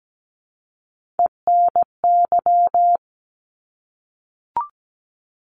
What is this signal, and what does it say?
Morse code, heard as a keyed tone, about 17 wpm, 700 Hz. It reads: ENY
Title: ARRL Sections 17wpm